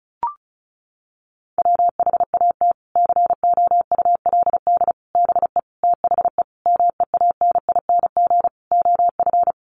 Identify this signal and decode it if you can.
Morse code, heard as a keyed tone, about 34 wpm, 700 Hz. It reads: WHAT COULD BE THE MEANING OF
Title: Sentences from Top 900 Words 35wpm